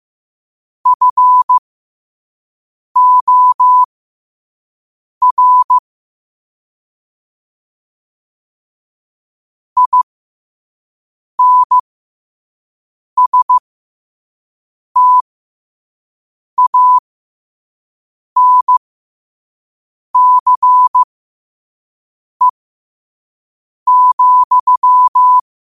Morse code, heard as a keyed tone, about 15 wpm, spaced as for 5 wpm, 1000 Hz. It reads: FOR INSTANCE,
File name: qotdfiles / 5wpm